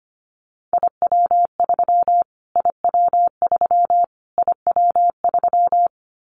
Morse code, heard as a keyed tone, about 25 wpm, 700 Hz. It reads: IW3 IW3 IW3
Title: Callsign Prefixes for Top 25 DXCC - Repeated 3x - 25wpm